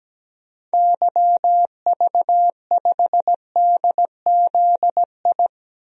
Morse code, heard as a keyed tone, about 17 wpm, 700 Hz. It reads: YV5DZI
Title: DXCC-100 Call Signs - 6 Characters - 17wpm